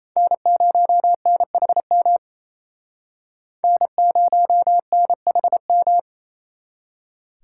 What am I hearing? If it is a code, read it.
Morse code, heard as a keyed tone, about 36 wpm, 700 Hz. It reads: N0NHM N0NHM